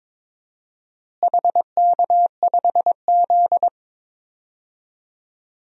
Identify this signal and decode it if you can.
Morse code, heard as a keyed tone, about 22 wpm, 700 Hz. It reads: HK5Z
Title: DXCC-100 Call Signs - 4 Characters - Repeated 2x - 22wpm